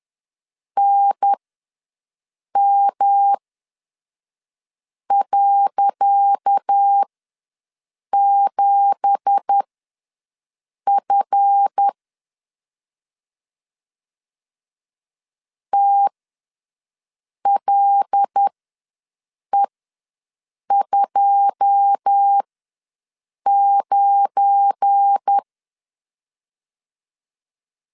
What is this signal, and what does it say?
Morse code, heard as a keyed tone, about 10 wpm, spaced as for 5 wpm, 800 Hz. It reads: NM.7F TLE29
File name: LEZIONI / 10 SIGNS / 10C DOT ws 10wpm 4213